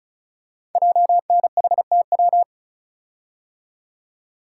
Morse code, heard as a keyed tone, about 36 wpm, 700 Hz. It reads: JNHTW
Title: Code Groups - 5 Letters - 35wpm